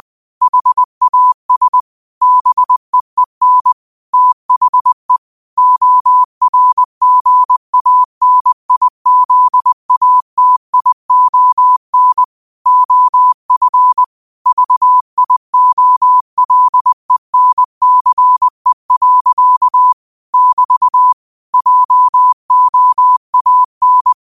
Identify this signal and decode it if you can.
Morse code, heard as a keyed tone, about 20 wpm, 1000 Hz. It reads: HAS BEEN THE ORGANIZATION OF VIOLENCE. <BT> JOAN
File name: qotdfiles / 20wpm